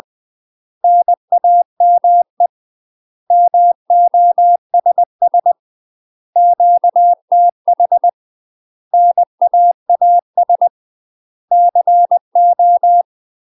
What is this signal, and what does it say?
Morse code, heard as a keyed tone, about 20 wpm, 700 Hz. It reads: NAME MOSS QTH NAAS CO
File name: UK MASTER QSOs TRAINING / 20WPM / UK MASTER QSOs 0001